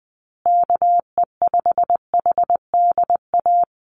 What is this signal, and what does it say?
Morse code, heard as a keyed tone, about 20 wpm, 700 Hz. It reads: KE5HDA
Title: USA Call Signs Repeated 2x 20wpm